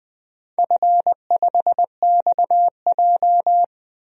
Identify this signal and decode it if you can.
Morse code, heard as a keyed tone, about 20 wpm, 700 Hz. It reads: F5XJ
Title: DXCC-100 Call Signs - 4 Characters - 20wpm